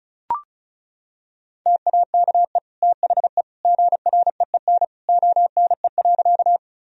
Morse code, heard as a keyed tone, about 34 wpm, 700 Hz. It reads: TAKE THE GREEN ONE.
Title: Sentences from Top 1000 Words - Review Part 2 - 35wpm